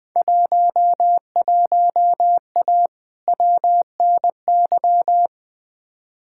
Morse code, heard as a keyed tone, about 20 wpm, 700 Hz. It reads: 11A WNY
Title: Field Day Exchanges Encore - 20wpm